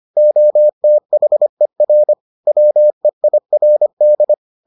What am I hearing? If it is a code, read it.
Morse code, heard as a keyed tone, about 25 wpm, 600 Hz. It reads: OTHER WEIRD